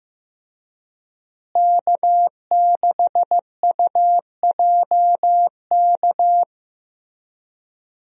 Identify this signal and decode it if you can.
Morse code, heard as a keyed tone, about 15 wpm, 700 Hz. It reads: K6UJK
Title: USA Call Signs - 5 Characters - Repeated 2x - 15wpm